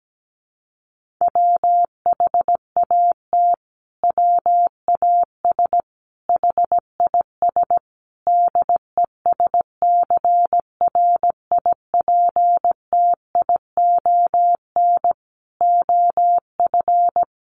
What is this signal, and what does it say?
Morse code, heard as a keyed tone, about 17 wpm, 700 Hz. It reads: WHAT WAS HIS DESCRIPTION OF
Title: Sentences from Top 1300 Words 17wpm